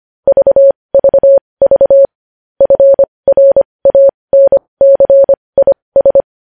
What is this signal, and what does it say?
Morse code, heard as a keyed tone, about 25 wpm, spaced as for 20 wpm, 550 Hz. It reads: VVV FRANCIS
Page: News Headlines in Morse Code at 20 WPM